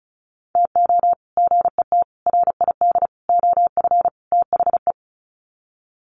Morse code, heard as a keyed tone, about 34 wpm, 700 Hz. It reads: TO GET RID OF THE
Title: Sets of 5 Words 35wpm